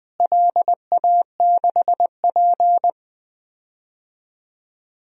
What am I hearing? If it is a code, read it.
Morse code, heard as a keyed tone, about 20 wpm, 700 Hz. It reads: LA6P
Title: DXCC-100 Call Signs - 4 Characters - Repeated 2x - 20wpm